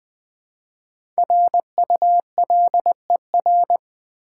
Morse code, heard as a keyed tone, about 20 wpm, 700 Hz. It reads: RULER
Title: Easy Words with Hints - 20wpm